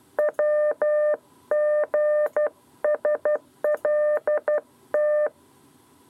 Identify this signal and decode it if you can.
Morse code, heard as a keyed tone, about 12 wpm, 600 Hz. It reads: WGSLT
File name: 7b lezione cw 2